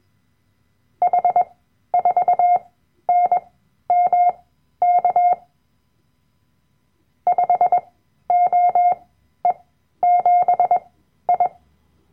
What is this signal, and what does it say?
Morse code, heard as a keyed tone, about 21 wpm, spaced as for 12 wpm, 710 Hz. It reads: H4NMK 5OE7I
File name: Workshop Dateien / CW-Online-Kurs / Lektion 010